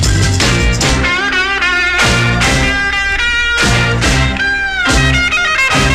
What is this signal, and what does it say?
Emergency Alert System (SAME) tones, heard as harsh buzzing bursts. It is heard from 1.0 to 4.8 s.